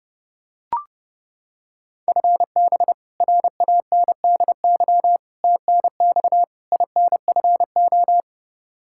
Morse code, heard as a keyed tone, about 32 wpm, 700 Hz. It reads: FB RANDY TNX INFO
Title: K1USN SST QSO Practice 30wpm